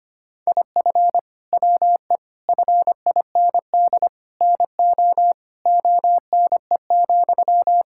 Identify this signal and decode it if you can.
Morse code, heard as a keyed tone, about 25 wpm, 700 Hz. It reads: IF WE FIND NO ONE,
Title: Sentences from Top 1600 Words 25wpm